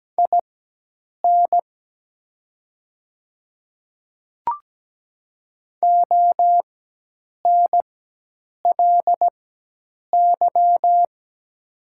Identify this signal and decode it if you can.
Morse code, heard as a keyed tone, about 17 wpm, spaced as for 8 wpm, 700 Hz. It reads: IN ONLY
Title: 4x Character Spacing - Top 200 Words - 17wpm